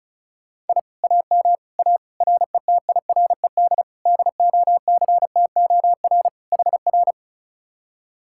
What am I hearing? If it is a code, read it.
Morse code, heard as a keyed tone, about 43 wpm, 700 Hz. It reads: I AM A RETIRED DOCTOR HR
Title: Speed-Racing Common QSO Phrases - 35wpm